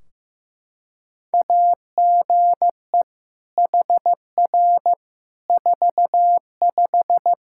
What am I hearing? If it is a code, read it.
Morse code, heard as a keyed tone, about 15 wpm, 700 Hz. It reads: AGE HR 45
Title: Practice QSO - Exchange 3 - More Info - 15wpm